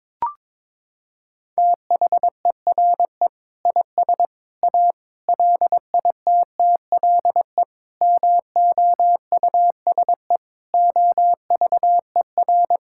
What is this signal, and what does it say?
Morse code, heard as a keyed tone, about 22 wpm, 700 Hz. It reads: THERE IS A LITTLE MOUSE OVER
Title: Sentences from Top 1000 Words - Review Part 5 - 22wpm